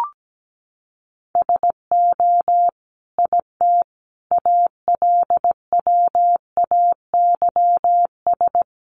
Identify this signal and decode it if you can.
Morse code, heard as a keyed tone, about 17 wpm, 700 Hz. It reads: SO IT ALWAYS
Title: Sentences from Top 300 Words 17wpm